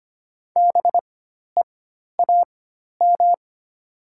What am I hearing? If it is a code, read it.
Morse code, heard as a keyed tone, about 25 wpm, spaced as for 12 wpm, 700 Hz. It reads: BEAM
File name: Top 90 QSO Elements - 4x Character Spacing - 25wpm